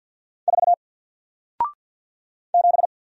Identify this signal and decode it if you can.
Morse code, heard as a keyed tone, about 48 wpm, 700 Hz. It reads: 4 7
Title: Single Character (Letter:Number:Punctuation:Prosign) Alt 50wpm